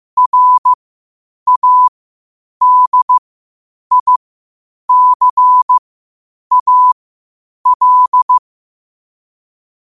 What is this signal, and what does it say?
Morse code, heard as a keyed tone, about 15 wpm, spaced as for 8 wpm, 1000 Hz. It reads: RADICAL
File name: qotdfiles / 8wpm